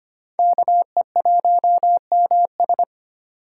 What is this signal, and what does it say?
Morse code, heard as a keyed tone, about 33 wpm, 700 Hz. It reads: KE1MS